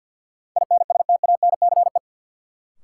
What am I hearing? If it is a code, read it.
Morse code, heard as a keyed tone, about 52 wpm, 700 Hz. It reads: INSTANCE
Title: Sentences from Top 800 Words - New Word Drill - 50wpm